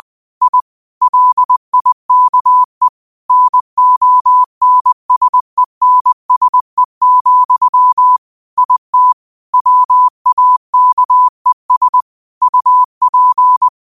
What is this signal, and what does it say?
Morse code, heard as a keyed tone, about 20 wpm, 1000 Hz. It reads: I LIKE NONSENSE, IT WAKES UP
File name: qotdfiles / 20wpm